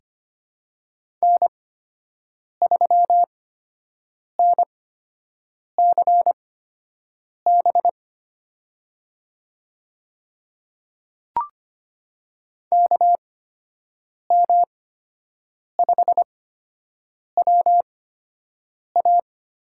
Morse code, heard as a keyed tone, about 25 wpm, spaced as for 7 wpm, 700 Hz. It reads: N3NCB KM5WA
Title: USA Call Signs - 5 Characters - 8x Character Spacing - 25wpm